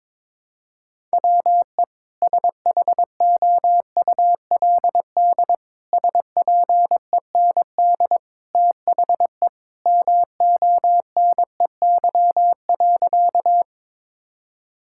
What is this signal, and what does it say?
Morse code, heard as a keyed tone, about 22 wpm, 700 Hz. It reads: WE SHOULD SPEND THE MONEY.
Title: Sentences from Top 1000 Words - Review Part 5 - 22wpm